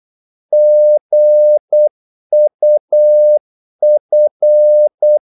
Morse code, heard as a keyed tone, about 8 wpm, 600 Hz. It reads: GUF
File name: Korte ord DK 8wpm